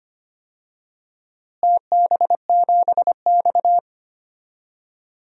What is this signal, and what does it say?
Morse code, heard as a keyed tone, about 25 wpm, 700 Hz. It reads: TB7X
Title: DXCC-100 Call Signs - 4 Characters - Repeated 2x - 25wpm